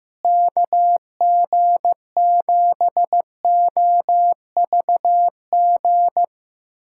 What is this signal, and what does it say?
Morse code, heard as a keyed tone, about 15 wpm, 700 Hz. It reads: KG7OVG
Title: USA Callsign DE Callsign - 6 Characters - 15wpm